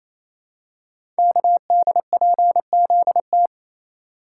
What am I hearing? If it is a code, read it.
Morse code, heard as a keyed tone, about 29 wpm, 700 Hz. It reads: KDPZT
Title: Code Groups - 5 Letters - 28wpm